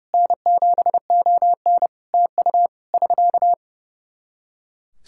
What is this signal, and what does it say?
Morse code, heard as a keyed tone, about 30 wpm, 700 Hz. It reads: N7ON TU <SK>
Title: Practice QSO - Exchange 4 - Ending - 30wpm